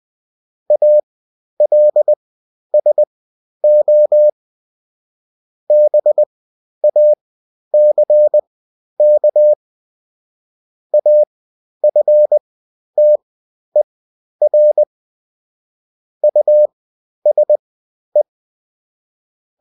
Morse code, heard as a keyed tone, about 20 wpm, spaced as for 11 wpm, 600 Hz. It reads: ALSO BACK AFTER USE
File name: Top-100-Words-10